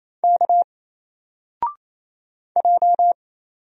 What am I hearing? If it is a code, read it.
Morse code, heard as a keyed tone, about 28 wpm, 700 Hz. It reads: K J
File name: Single-Letter-Number-28wpm